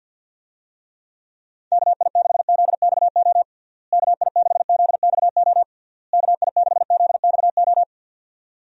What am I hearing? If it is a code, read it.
Morse code, heard as a keyed tone, about 51 wpm, 700 Hz. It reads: KI6ZXO KI6ZXO KI6ZXO
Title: USA Call Signs - 6 Characters - Repeated 3x - 50wpm